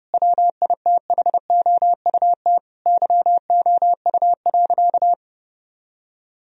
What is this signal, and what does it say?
Morse code, heard as a keyed tone, about 30 wpm, 700 Hz. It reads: WITHOUT YOU.
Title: Sentences from Top 1000 Words - Review Part 3 - 30wpm